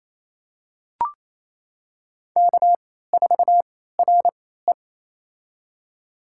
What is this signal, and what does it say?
Morse code, heard as a keyed tone, about 28 wpm, spaced as for 16 wpm, 700 Hz. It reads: K4RE
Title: USA Call Signs - 4 Characters - 3x Character Spacing - 28wpm